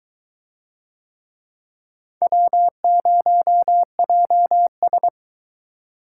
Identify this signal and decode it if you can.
Morse code, heard as a keyed tone, about 20 wpm, 700 Hz. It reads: W0JS